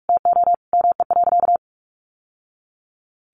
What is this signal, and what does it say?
Morse code, heard as a keyed tone, about 44 wpm, 700 Hz. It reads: TO ME.
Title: Sentences from Top 1200 Words 45wpm